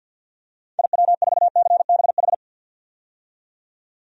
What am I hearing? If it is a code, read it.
Morse code, heard as a keyed tone, about 46 wpm, 700 Hz. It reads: IW4CBH
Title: DXCC-100 Call Signs - 6 Characters - 50wpm